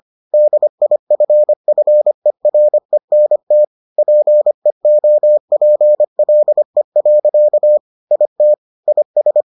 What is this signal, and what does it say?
Morse code, heard as a keyed tone, about 25 wpm, 600 Hz. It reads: DIFFERENT PEOPLE. IT IS